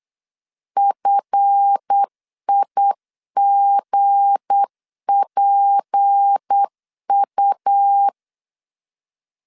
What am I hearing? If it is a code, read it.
Morse code, heard as a keyed tone, about 8 wpm, 800 Hz. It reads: FIGPU
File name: LEZIONI / 7 GF9Z / 7 GF 8wpm 3108